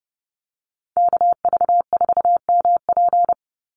Morse code, heard as a keyed tone, about 30 wpm, 700 Hz. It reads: KV4MP
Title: USA Call Signs 30wpm